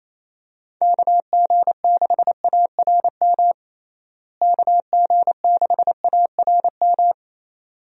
Morse code, heard as a keyed tone, about 28 wpm, 700 Hz. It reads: KG6ARM KG6ARM
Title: USA Call Signs - 6 Characters - Repeated 2x - 28wpm